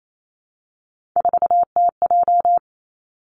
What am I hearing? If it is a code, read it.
Morse code, heard as a keyed tone, about 28 wpm, 700 Hz. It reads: 4TJ
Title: Three Challenging Letter-Number Combinations 28wpm